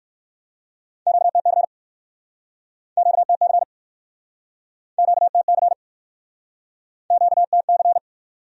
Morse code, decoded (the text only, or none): QTC QTC QTC QTC